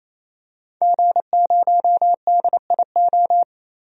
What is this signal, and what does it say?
Morse code, heard as a keyed tone, about 28 wpm, 700 Hz. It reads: G0DIO
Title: DXCC-100 Call Signs - 5 Characters - 28wpm